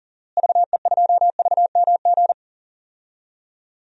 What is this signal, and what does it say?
Morse code, heard as a keyed tone, about 40 wpm, 700 Hz. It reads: VE2VMG
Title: DXCC-100 Call Signs - 6 Characters - Repeated 3x - 40wpm